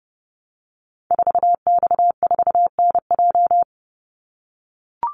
Morse code, heard as a keyed tone, about 30 wpm, 700 Hz. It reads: 4X4NJ
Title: Call Signs - 30wpm